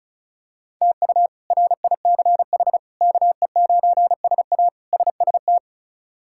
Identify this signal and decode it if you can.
Morse code, heard as a keyed tone, about 36 wpm, 700 Hz. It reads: TU RICH KE9SA SST
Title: K1USN SST QSO Practice - 35wpm